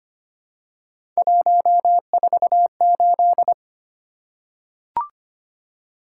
Morse code, heard as a keyed tone, about 25 wpm, 700 Hz. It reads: 148
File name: Three Letter-Number Combinations - 25wpm